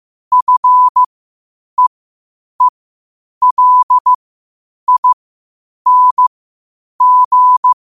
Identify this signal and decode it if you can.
Morse code, heard as a keyed tone, about 15 wpm, spaced as for 8 wpm, 1000 Hz. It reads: FEELING
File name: qotdfiles / 8wpm